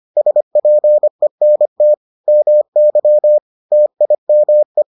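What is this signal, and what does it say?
Morse code, heard as a keyed tone, about 25 wpm, 600 Hz. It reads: SPENT MY TIME
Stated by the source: Morse edition of the Foundations of Amateur Radio episodes.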